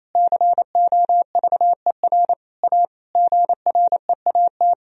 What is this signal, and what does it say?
Morse code, heard as a keyed tone, about 27 wpm, 700 Hz. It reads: COVER A GREAT
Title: Sentences from Top 1000 Words - Review Part 2 - 28wpm